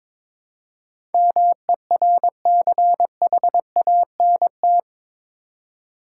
Morse code, heard as a keyed tone, about 22 wpm, 700 Hz. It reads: MERCHANT
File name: practice / 22wpm / Eight Letter Words - 22wpm